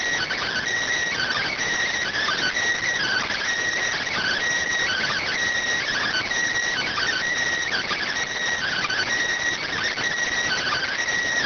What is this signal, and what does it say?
Emergency Alert System (SAME) tones (header bursts), heard as harsh buzzing bursts. It is heard from 1.1 to 11.0 s.